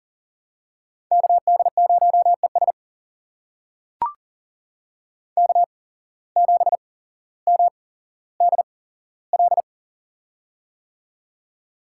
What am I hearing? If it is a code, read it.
Morse code, heard as a keyed tone, about 40 wpm, spaced as for 17 wpm, 700 Hz. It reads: KD0ES K7MDL
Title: USA Call Signs - 8x Character Spacing - 40wpm